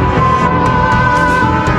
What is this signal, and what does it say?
There is no code.